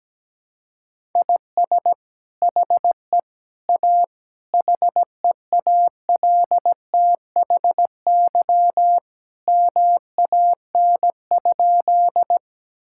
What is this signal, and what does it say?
Morse code, heard as a keyed tone, about 17 wpm, 700 Hz. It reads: IS HE A HEALTHY MAN?
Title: Sentences from Top 1000 Words - Review Part 5 - 17wpm